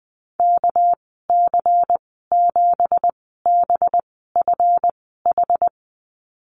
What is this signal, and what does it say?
Morse code, heard as a keyed tone, about 20 wpm, spaced as for 15 wpm, 700 Hz. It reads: KC7BFH
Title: 213 Lesson 34g - Number 8 - Call Signs 20wpm